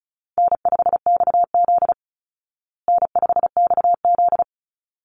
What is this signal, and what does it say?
Morse code, heard as a keyed tone, about 34 wpm, 700 Hz. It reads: N5XZ N5XZ
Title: Easy Call Signs Repeated 2x - 35wpm